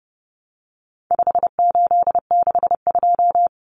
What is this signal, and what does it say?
Morse code, heard as a keyed tone, about 30 wpm, 700 Hz. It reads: 5862
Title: Common Cut Numbers - 4-Digit 30wpm